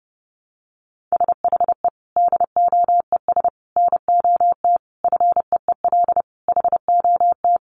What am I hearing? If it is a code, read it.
Morse code, heard as a keyed tone, about 30 wpm, 700 Hz. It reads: SHE DOES NOT FEEL HOT
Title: Additional Easy Phrases 30wpm